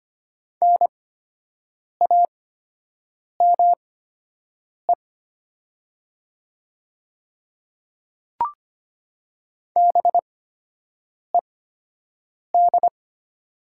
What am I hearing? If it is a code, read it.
Morse code, heard as a keyed tone, about 25 wpm, spaced as for 7 wpm, 700 Hz. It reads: NAME BED
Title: 8x Character Spacing - Top 500 Words - 25wpm